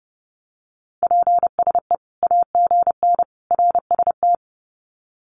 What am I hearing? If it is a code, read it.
Morse code, heard as a keyed tone, about 30 wpm, 700 Hz. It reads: PSE AGN RST
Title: Common QSO Phrases 30wpm